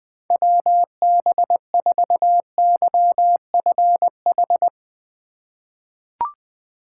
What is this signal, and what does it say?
Morse code, heard as a keyed tone, about 20 wpm, 700 Hz. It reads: WB4YFH
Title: USA Call Signs - 2x Character Spacing - 20wpm